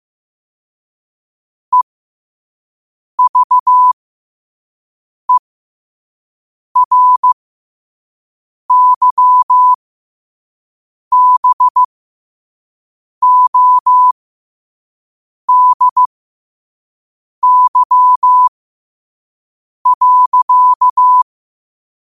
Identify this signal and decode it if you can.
Morse code, heard as a keyed tone, about 15 wpm, spaced as for 5 wpm, 1000 Hz. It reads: EVERYBODY.
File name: qotdfiles / 5wpm